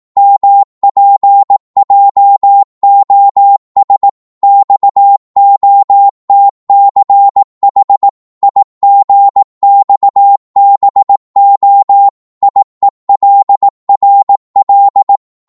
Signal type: Morse code